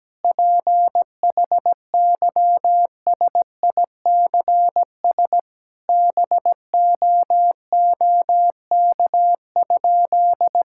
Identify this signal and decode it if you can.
Morse code, heard as a keyed tone, about 17 wpm, 700 Hz. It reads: PHYSICS BOOK?
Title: Sentences from Top 1000 Words - Review Part 5 - 17wpm